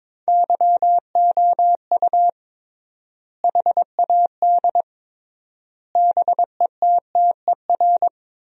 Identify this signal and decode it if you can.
Morse code, heard as a keyed tone, about 21 wpm, 700 Hz. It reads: YOU HAD BETTER
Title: 3x Word-Spacing - Sentences from Top 400 Words - 22wpm